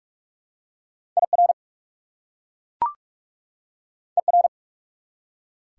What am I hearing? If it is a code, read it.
Morse code, heard as a keyed tone, about 42 wpm, 690 Hz. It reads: IR ER
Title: Prefixes and Suffixes 45wpm